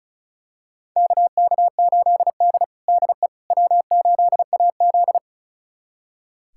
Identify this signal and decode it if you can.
Morse code, heard as a keyed tone, about 36 wpm, 700 Hz. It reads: KK8D DE W8AZ
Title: USA Callsign DE Callsign - 4 Characters - 35wpm